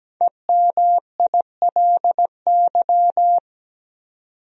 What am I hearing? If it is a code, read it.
Morse code, heard as a keyed tone, about 17 wpm, 700 Hz. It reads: EMILY
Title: USA Common Names - 17wpm